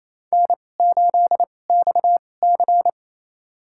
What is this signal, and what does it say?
Morse code, heard as a keyed tone, about 28 wpm, spaced as for 20 wpm, 700 Hz. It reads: N8XC
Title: USA Call Signs - 2x Character Spacing - 28wpm